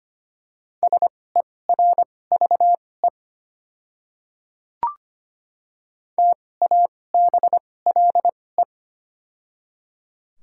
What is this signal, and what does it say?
Morse code, heard as a keyed tone, about 25 wpm, spaced as for 18 wpm, 700 Hz. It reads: SERVE TABLE
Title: 2x Character Spacing - Top 500 Words - 25wpm